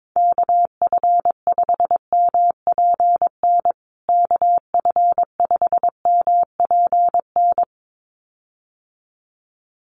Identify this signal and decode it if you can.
Morse code, heard as a keyed tone, about 22 wpm, 700 Hz. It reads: KF5MPN KF5MPN
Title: USA Call Signs Repeated 2x 22wpm